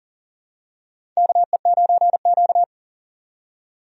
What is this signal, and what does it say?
Morse code, heard as a keyed tone, about 40 wpm, 700 Hz. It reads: KE9Q
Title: USA Call Signs - 4 Characters - 40wpm